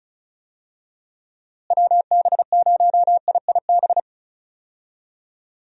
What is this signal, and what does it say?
Morse code, heard as a keyed tone, about 36 wpm, 700 Hz. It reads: WB0IIB
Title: USA Call Signs - 6 Characters - Repeated 3x - 35wpm